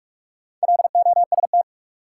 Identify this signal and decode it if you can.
Morse code, heard as a keyed tone, about 47 wpm, 700 Hz. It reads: LOST
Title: Sentences from Top 1400 Words - New Word Drill Alt - 45wpm